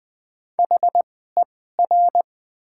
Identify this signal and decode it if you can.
Morse code, heard as a keyed tone, about 20 wpm, spaced as for 14 wpm, 700 Hz. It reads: HER
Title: 2x Character Spacing - Top 300 Words - 20wpm